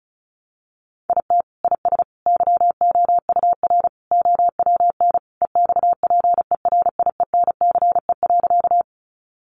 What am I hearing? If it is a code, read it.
Morse code, heard as a keyed tone, about 34 wpm, 700 Hz. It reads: IT IS YOUR OWN EXPERIENCE.
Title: Sentences from Top 600 Words 35wpm